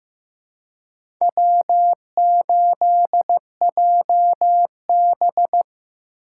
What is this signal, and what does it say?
Morse code, heard as a keyed tone, about 15 wpm, 700 Hz. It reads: W8JB
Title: USA Call Signs - 4 Characters - 3x Character Spacing - 15wpm